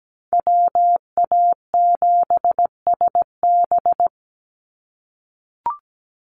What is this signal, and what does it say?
Morse code, heard as a keyed tone, about 17 wpm, 700 Hz. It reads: WA7SB
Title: NA SOTA Chaser Callsigns 17wpm